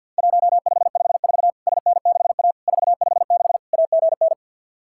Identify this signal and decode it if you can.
Morse code, heard as a keyed tone, about 50 wpm, 700 Hz. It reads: 1554 SN6A 456 AGN
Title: Worked All Europe DX Contest - QTC Exchange with Repeats - 50wpm